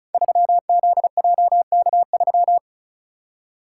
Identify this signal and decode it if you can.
Morse code, heard as a keyed tone, about 34 wpm, 700 Hz. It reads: 3ZJK3
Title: Code Groups - 5 Letter:Numbers - 35wpm